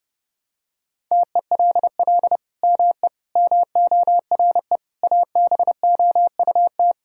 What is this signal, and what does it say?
Morse code, heard as a keyed tone, about 30 wpm, 700 Hz. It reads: TELL ME MORE ABOUT
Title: Sentences from Top 1000 Words - Review Part 6 - 30wpm